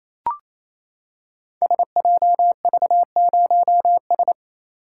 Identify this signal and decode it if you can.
Morse code, heard as a keyed tone, about 28 wpm, 700 Hz. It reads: SJV0S
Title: Code Groups - 5 Letter:Numbers - 28wpm